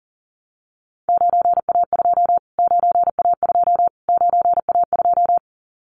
Morse code, heard as a keyed tone, about 40 wpm, 700 Hz. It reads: 9A2 9A2 9A2
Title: Callsign Prefixes for Top 25 DXCC - Repeated 3x - 40wpm